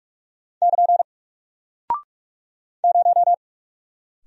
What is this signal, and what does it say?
Morse code, heard as a keyed tone, about 43 wpm, 700 Hz. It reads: <KN> 0
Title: Single Character (Letter:Number:Punctuation:Prosign) Alt 45wpm